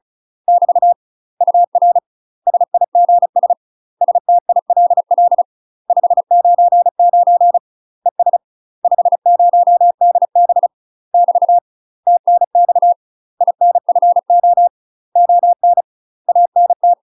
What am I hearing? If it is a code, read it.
Morse code, heard as a keyed tone, about 36 wpm, 700 Hz. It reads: <BT> UR SIGS STILL 599 ES 50DB <BT> TNX INFO ON ANT
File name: UK MASTER QSOs TRAINING / 35WPM / UK MASTER QSOs 0023